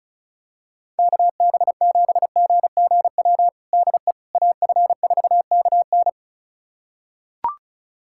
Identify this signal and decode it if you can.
Morse code, heard as a keyed tone, about 34 wpm, 700 Hz. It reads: KB7GGW DE AF4KN
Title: USA Callsign DE Callsign 35wpm